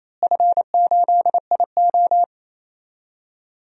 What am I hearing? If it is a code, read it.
Morse code, heard as a keyed tone, about 28 wpm, 700 Hz. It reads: F8IO
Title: DXCC-100 Call Signs - 4 Characters - Repeated 3x - 28wpm